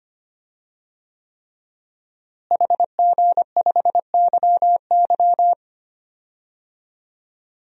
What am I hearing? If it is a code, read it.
Morse code, heard as a keyed tone, about 25 wpm, 700 Hz. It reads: HG5YY
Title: DXCC-100 Call Signs - 5 Characters - Repeated 3x - 25wpm